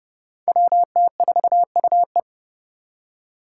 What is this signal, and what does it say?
Morse code, heard as a keyed tone, about 30 wpm, 700 Hz. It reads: WT4UE
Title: Five Challenging Letter-Number Combinations 30wpm